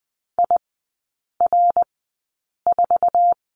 Morse code, heard as a keyed tone, about 20 wpm, spaced as for 8 wpm, 700 Hz. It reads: IR4
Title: 138 Lesson 23e - Letter f - Three Characters 20wpm